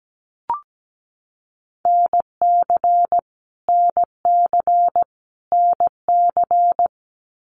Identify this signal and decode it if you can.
Morse code, heard as a keyed tone, about 17 wpm, 700 Hz. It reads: NC NC NC
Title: US States Repeated 3x 17wpm